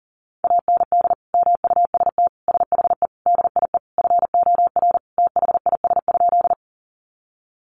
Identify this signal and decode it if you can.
Morse code, heard as a keyed tone, about 40 wpm, 700 Hz. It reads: AND MUST SHE DIE FOR THIS?
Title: Sentences from Top 1300 Words 40wpm